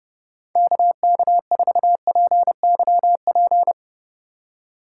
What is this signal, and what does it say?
Morse code, heard as a keyed tone, about 36 wpm, 700 Hz. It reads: KK4PYP
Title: Speed Racing - USA Call Signs 30wpm